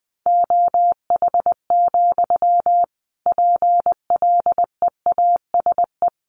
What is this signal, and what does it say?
Morse code, heard as a keyed tone, about 20 wpm, 700 Hz. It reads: OH, PLEASE
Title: Sentences from Top 1500 Words 20wpm